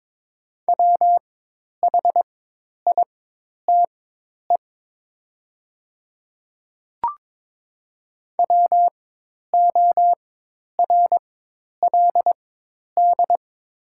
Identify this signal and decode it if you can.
Morse code, heard as a keyed tone, about 22 wpm, spaced as for 10 wpm, 700 Hz. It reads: WHITE WORLD
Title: 4x Character Spacing - Top 500 Words - 22wpm